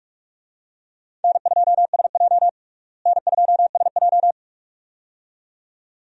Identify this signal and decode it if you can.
Morse code, heard as a keyed tone, about 46 wpm, 700 Hz. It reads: N2SJ N2SJ
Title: USA Call Signs - 4 Characters - Repeated 2x - 45wpm